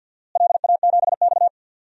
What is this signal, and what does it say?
Morse code, heard as a keyed tone, about 51 wpm, 700 Hz. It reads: LA7X
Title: DXCC-100 Call Signs - 4 Characters - Repeated 2x - 50wpm